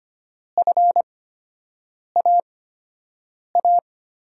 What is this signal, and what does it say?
Morse code, heard as a keyed tone, about 25 wpm, spaced as for 7 wpm, 700 Hz. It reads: FAA
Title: USA Call Signs - 6 Characters - 8x Character Spacing - 25wpm